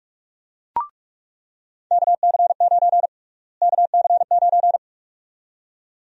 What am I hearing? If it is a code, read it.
Morse code, heard as a keyed tone, about 41 wpm, 700 Hz. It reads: KC9 KC9
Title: USA Call Sign Prefixes - Repeated 2x - 45wpm